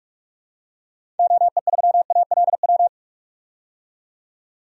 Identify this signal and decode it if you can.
Morse code, heard as a keyed tone, about 43 wpm, 700 Hz. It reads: OE3ALW
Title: DXCC-100 Call Signs - 6 Characters - 45wpm